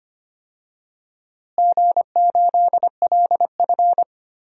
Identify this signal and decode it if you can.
Morse code, heard as a keyed tone, about 25 wpm, 700 Hz. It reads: G8LF
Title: DXCC-100 Call Signs - 4 Characters - Repeated 3x - 25wpm